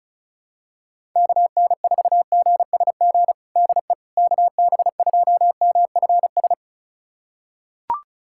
Morse code, heard as a keyed tone, about 34 wpm, 700 Hz. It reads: KN4GSG DE KB2MFS
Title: USA Callsign DE Callsign - 6 Characters - 35wpm